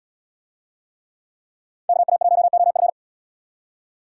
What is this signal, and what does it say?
Morse code, heard as a keyed tone, about 70 wpm, 700 Hz. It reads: KA1OF